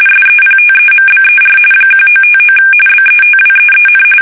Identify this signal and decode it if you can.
RTTY, about 45.45 baud. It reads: SEPTEMBER 1959 BETWEEN
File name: fsk850 / teletype-RTTY